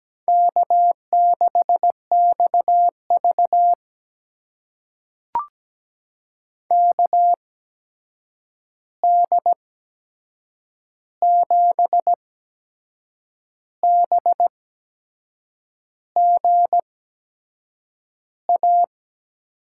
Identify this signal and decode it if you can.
Morse code, heard as a keyed tone, about 17 wpm, spaced as for 6 wpm, 700 Hz. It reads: K6XV KD7BGA
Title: USA Call Signs - 8x Character Spacing - 17wpm